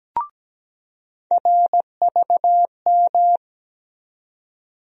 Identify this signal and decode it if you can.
Morse code, heard as a keyed tone, about 17 wpm, 700 Hz. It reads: RVM
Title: USA Call Sign Postfixes - 3 Letters 17wpm